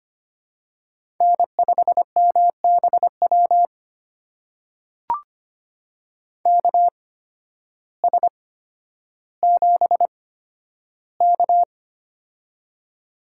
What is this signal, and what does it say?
Morse code, heard as a keyed tone, about 25 wpm, spaced as for 12 wpm, 700 Hz. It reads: N5MBW KS7K
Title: USA Call Signs - 8x Character Spacing - 25wpm